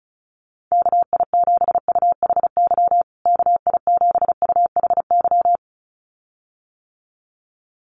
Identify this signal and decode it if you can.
Morse code, heard as a keyed tone, about 34 wpm, 700 Hz. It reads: KI7UHY KI7UHY
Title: USA Call Signs Repeated 2x 35wpm